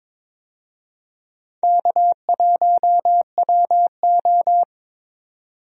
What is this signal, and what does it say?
Morse code, heard as a keyed tone, about 21 wpm, 700 Hz. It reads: K1WO
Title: USA Call Signs - 4 Characters - Repeated 2x - 22wpm